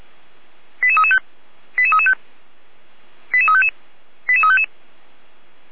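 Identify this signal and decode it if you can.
Five-tone selective call (ZVEI), heard as quick runs of five tones, tones of about 70 ms. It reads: 88335 88335 88355 88355
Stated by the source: ZVEI five-tone selective call received on SDR.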